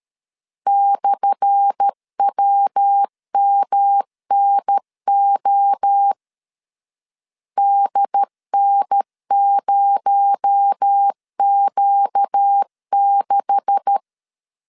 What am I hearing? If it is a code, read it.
Morse code, heard as a keyed tone, about 13 wpm, 800 Hz. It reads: /WMNO DN0Q6
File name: LEZIONI / 10 SIGNS / 10A SLASH 12wpm 2314